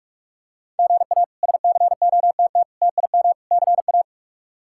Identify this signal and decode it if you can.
Morse code, heard as a keyed tone, about 46 wpm, 700 Hz. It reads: GA SCOTT TIM CA
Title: K1USN SST QSO Practice - 45wpm